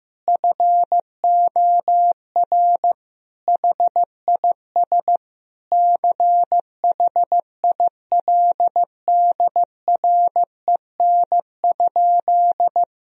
Morse code, heard as a keyed tone, about 15 wpm, 700 Hz. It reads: FOR HIS CHILDREN?
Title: Sentences from Top 1000 Words - Review Part 3 - 15wpm